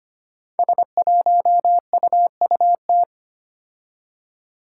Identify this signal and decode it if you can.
Morse code, heard as a keyed tone, about 25 wpm, 700 Hz. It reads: S1UUT
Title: Five Challenging Letter-Number Combinations 25wpm